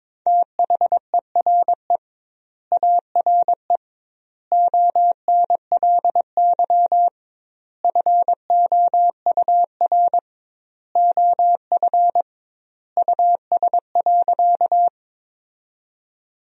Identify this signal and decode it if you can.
Morse code, heard as a keyed tone, about 22 wpm, 700 Hz. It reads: THERE ARE ONLY FOUR OF US.
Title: 2x Word-Spacing - Sentences from Top 300 Words - 22wpm